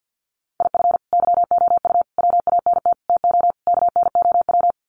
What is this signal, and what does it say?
Morse code, heard as a keyed tone, about 50 wpm, 700 Hz. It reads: IF YOU WANT TO KNOW
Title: Sets of 5 Words Encore2 50wpm